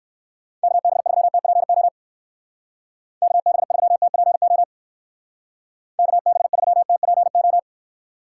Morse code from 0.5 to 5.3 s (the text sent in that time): KB3TPO KB3TPO